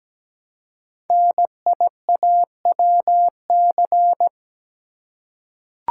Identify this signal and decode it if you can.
Morse code, heard as a keyed tone, about 17 wpm, 700 Hz. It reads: NIAWC
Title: Code Groups - 5 Letters - 17wpm